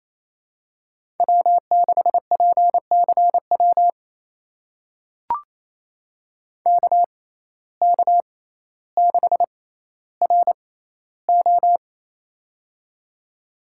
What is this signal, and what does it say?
Morse code, heard as a keyed tone, about 28 wpm, spaced as for 14 wpm, 700 Hz. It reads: W6PCW KK6RO
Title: USA Call Signs - 5 Characters - 6x Character Spacing - 28wpm